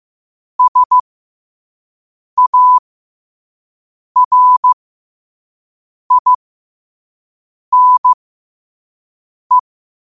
Morse code, heard as a keyed tone, about 15 wpm, spaced as for 5 wpm, 1000 Hz. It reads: SARINE
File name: qotdfiles / 5wpm